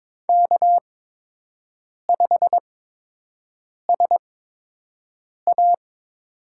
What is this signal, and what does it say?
Morse code, heard as a keyed tone, about 22 wpm, spaced as for 6 wpm, 700 Hz. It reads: K5SA
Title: USA Call Signs - 5 Characters - 8x Character Spacing - 22wpm